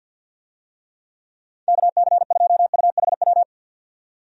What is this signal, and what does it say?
Morse code, heard as a keyed tone, about 51 wpm, 700 Hz. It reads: KC2UHW
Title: USA Call Signs - 6 Characters - 50wpm